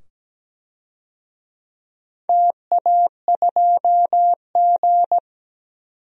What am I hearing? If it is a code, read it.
Morse code, heard as a keyed tone, about 17 wpm, 700 Hz. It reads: TA2G
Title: DXCC-100 Call Signs - 4 Characters - 17wpm